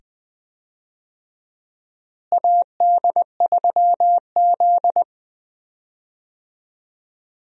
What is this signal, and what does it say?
Morse code, heard as a keyed tone, about 20 wpm, 700 Hz. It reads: AD3Z
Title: USA Call Signs - 4 Characters - Repeated 2x - 20wpm